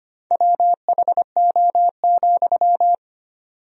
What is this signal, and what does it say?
Morse code, heard as a keyed tone, about 25 wpm, 700 Hz. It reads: WHO,
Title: 4x Word-Spacing - Sentences from Top 100 Words - 25wpm